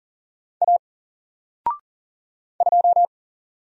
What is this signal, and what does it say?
Morse code, heard as a keyed tone, about 40 wpm, 700 Hz. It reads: A 2
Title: Single Letter-Number Alt 40wpm